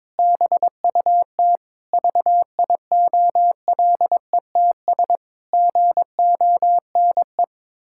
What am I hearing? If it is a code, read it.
Morse code, heard as a keyed tone, about 22 wpm, 700 Hz. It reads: BUT VIOLETS GONE
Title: Pangram Challenge v2 - 22wpm